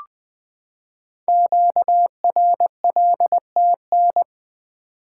Morse code, heard as a keyed tone, about 20 wpm, 700 Hz. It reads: QRLTN
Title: Code Groups - 5 Letters - 20wpm